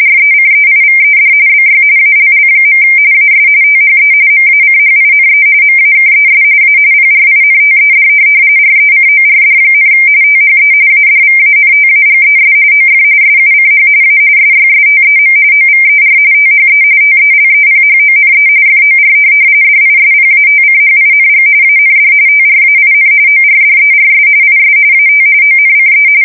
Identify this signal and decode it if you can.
RTTY, about 45.45 baud. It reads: RECEIVED BY ALL. AS THE MACHINE MAKES A PRINTED RECORD, THE ADVANTAGES OF SUCH A SYSTEM OVER THE TELEPHONE FOR DISPATCHING AND TRAIN ORDERS IS AT ONCE
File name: fsk170 / printing-telegraph